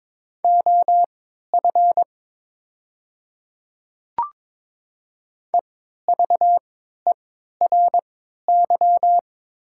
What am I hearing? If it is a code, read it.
Morse code, heard as a keyed tone, about 22 wpm, spaced as for 12 wpm, 700 Hz. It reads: OF EVERY
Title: 3x Character Spacing - Top 200 Words - 22wpm